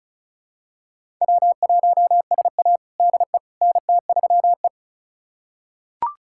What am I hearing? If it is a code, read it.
Morse code, heard as a keyed tone, about 36 wpm, 700 Hz. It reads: W1SA DE NT3E
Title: USA Callsign DE Callsign - 4 Characters - 35wpm